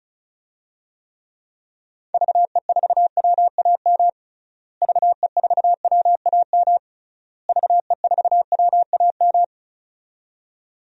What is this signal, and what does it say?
Morse code, heard as a keyed tone, about 36 wpm, 700 Hz. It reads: VE4WAM VE4WAM VE4WAM
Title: DXCC-100 Call Signs - 6 Characters - Repeated 3x - 35wpm